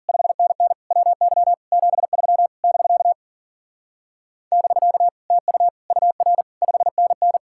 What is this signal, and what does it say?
Morse code, heard as a keyed tone, about 50 wpm, 700 Hz. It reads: 5NN WY 73 <BK> <BK> TU UR 5NN
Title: Speed-Racing Common QSO Phrases Encore - 40wpm